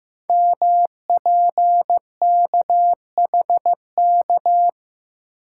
Morse code, heard as a keyed tone, about 15 wpm, 700 Hz. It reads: MPKHK
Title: Code Groups - 5 Letter:Numbers - 15wpm